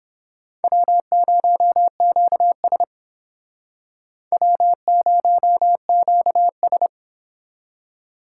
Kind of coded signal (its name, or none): Morse code